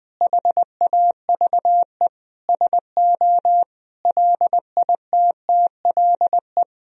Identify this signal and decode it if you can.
Morse code, heard as a keyed tone, about 20 wpm, 700 Hz. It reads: HAVE SO LITTLE
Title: Sentences from Top 1000 Words - Review Part 1 - 20wpm